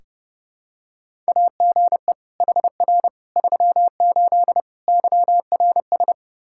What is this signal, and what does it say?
Morse code, heard as a keyed tone, about 30 wpm, 700 Hz. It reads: AGE HR 38 YRS
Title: Practice QSO - Exchange 3 - More Info - 30wpm